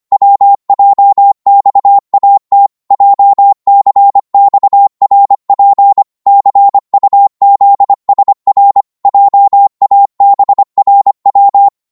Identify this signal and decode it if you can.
Morse code, heard as a keyed tone, about 25 wpm, 800 Hz. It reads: WJXAT JCXRP CUZSR JABRW